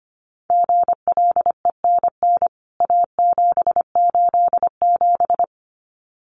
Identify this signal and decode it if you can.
Morse code, heard as a keyed tone, about 25 wpm, 700 Hz. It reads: GLENN A787
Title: CWT Exchange 25wpm